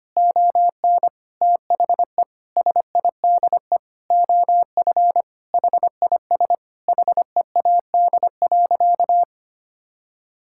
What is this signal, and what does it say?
Morse code, heard as a keyed tone, about 25 wpm, 700 Hz. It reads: ON THE SIDE OF HIS HEAD.
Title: Sentences from Top 1600 Words 25wpm